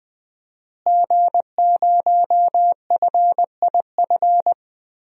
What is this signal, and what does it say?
Morse code, heard as a keyed tone, about 20 wpm, 700 Hz. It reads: G0FIF
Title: DXCC-100 Call Signs - 5 Characters - Repeated 3x - 20wpm